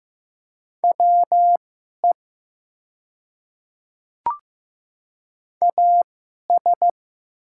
Morse code, heard as a keyed tone, about 15 wpm, spaced as for 11 wpm, 700 Hz. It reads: WE AS
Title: 2x Character Spacing - Top 100 Words - 15wpm